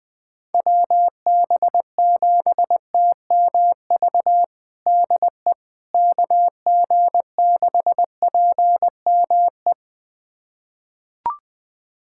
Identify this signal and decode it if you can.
Morse code, heard as a keyed tone, about 21 wpm, 700 Hz. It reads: WB7TMV DE KG6PME
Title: USA Callsign DE Callsign 20wpm